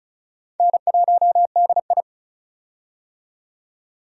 Morse code, heard as a keyed tone, about 34 wpm, 700 Hz. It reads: N1DI
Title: USA Call Signs - 4 Characters - 35wpm